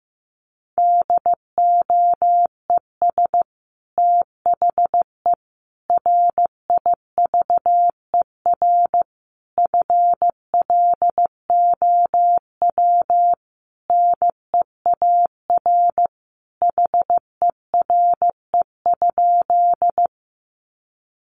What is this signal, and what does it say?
Morse code, heard as a keyed tone, about 15 wpm, 700 Hz. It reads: DOES THE RIVER FLOW NEAR HERE?
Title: Sentences from Top 1200 Words 15wpm